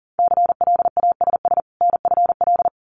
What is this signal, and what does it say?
Morse code, heard as a keyed tone, about 42 wpm, 700 Hz. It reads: CLASS NFL
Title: Winter Field Day Exchanges 40wpm